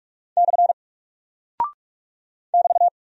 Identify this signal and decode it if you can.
Morse code, heard as a keyed tone, about 44 wpm, 700 Hz. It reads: / <BT>
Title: Single Character (Letter:Number:Punctuation:Prosign) 45wpm